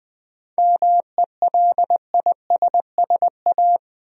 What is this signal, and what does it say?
Morse code, heard as a keyed tone, about 20 wpm, 700 Hz. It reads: MELISSA
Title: USA Common Names - 20wpm